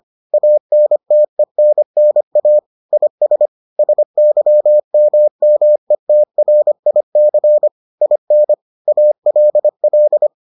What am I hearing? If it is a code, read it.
Morse code, heard as a keyed tone, about 25 wpm, 600 Hz. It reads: ANTENNA IS SYMMETRIC IN ALL